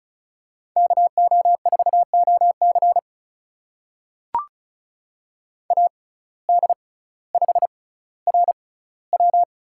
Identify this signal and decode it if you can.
Morse code, heard as a keyed tone, about 36 wpm, spaced as for 18 wpm, 700 Hz. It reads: KO4OC AD5RW
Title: USA Call Signs - 6x Character Spacing - 35wpm